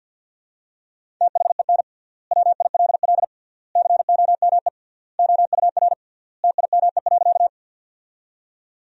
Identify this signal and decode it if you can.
Morse code, heard as a keyed tone, about 52 wpm, 700 Hz. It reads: THEN WILL COME OUR TIME.
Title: 3x Word-Spacing - Sentences from Top 200 Words - 50wpm